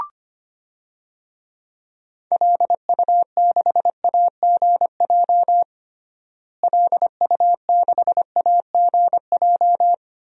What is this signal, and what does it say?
Morse code, heard as a keyed tone, about 25 wpm, 690 Hz. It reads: LU6AGJ LU6AGJ
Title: DXCC-100 Call Signs - 6 Characters - Repeated 3x - 25wpm